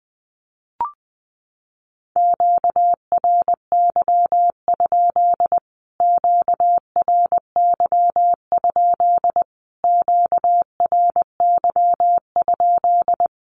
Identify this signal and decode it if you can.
Morse code, heard as a keyed tone, about 20 wpm, 700 Hz. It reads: QRY? QRY? QRY?
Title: Q-Codes Repeated 3x 20wpm